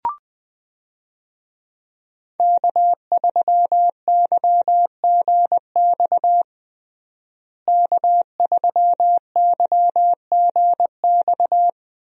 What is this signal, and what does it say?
Morse code, heard as a keyed tone, about 20 wpm, 700 Hz. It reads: K3YGX K3YGX
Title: USA Call Signs - 5 Characters - Repeated 2x - 20wpm